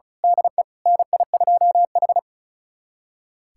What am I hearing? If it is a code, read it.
Morse code, heard as a keyed tone, about 36 wpm, 700 Hz. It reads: DE NI2H
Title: USA Callsign DE Callsign - 4 Characters - 35wpm